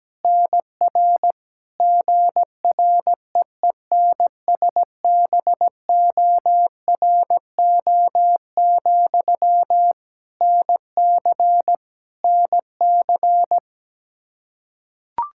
Morse code, heard as a keyed tone, about 17 wpm, 700 Hz. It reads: NR GREENSBORO, NC NC
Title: Practice QSO - Exchange 1 - Beginning - 17wpm